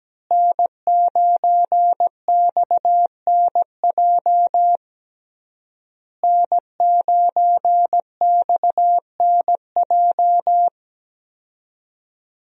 Morse code, heard as a keyed tone, about 17 wpm, 700 Hz. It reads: N9XNJ N9XNJ
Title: USA Call Signs - 5 Characters - Repeated 3x - 17wpm